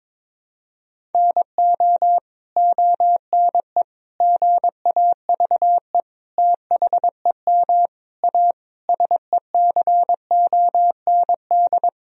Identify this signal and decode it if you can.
Morse code, heard as a keyed tone, about 22 wpm, 700 Hz. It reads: NO ONE GAVE THEM A SECOND
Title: Sentences from Top 1000 Words - Review Part 1 - 22wpm